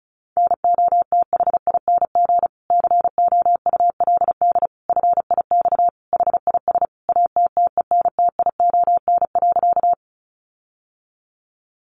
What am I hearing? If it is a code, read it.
Morse code, heard as a keyed tone, about 34 wpm, 700 Hz. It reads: NOTHING COULD FIX HIS ATTENTION.
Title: Sentences from Top 1400 Words 35wpm